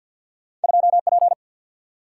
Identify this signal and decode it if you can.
Morse code, heard as a keyed tone, about 53 wpm, 700 Hz. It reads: 2 P
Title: Single-Letter Single-Number 50wpm